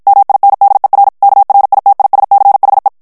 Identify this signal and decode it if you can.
Morse code, heard as a keyed tone, about 55 wpm, 800 Hz. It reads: MINDER KRITISCHE
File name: HOLLANDAIS texte clair 2 vitesse 50 mots minute